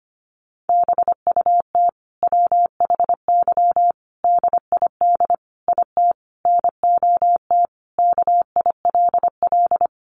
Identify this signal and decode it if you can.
Morse code, heard as a keyed tone, about 25 wpm, 700 Hz. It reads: BUT WHY DID IT NOT KILL
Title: Sentences from Top 1400 Words 25wpm